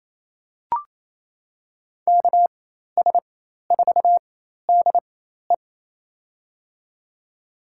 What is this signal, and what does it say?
Morse code, heard as a keyed tone, about 28 wpm, spaced as for 13 wpm, 700 Hz. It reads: KS4DE
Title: USA Call Signs - 5 Characters - 4x Character Spacing - 28wpm